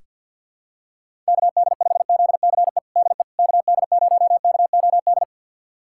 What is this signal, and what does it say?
Morse code, heard as a keyed tone, about 46 wpm, 700 Hz. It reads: KD5ZCE DE KD0KOD
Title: USA Callsign DE Callsign - 6 Characters - 50wpm